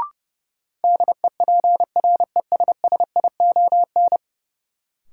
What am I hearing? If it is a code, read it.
Morse code, heard as a keyed tone, about 30 wpm, 700 Hz. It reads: DEPRESSION
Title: Sentences from Top 1300 Words - New Word Drill - 30wpm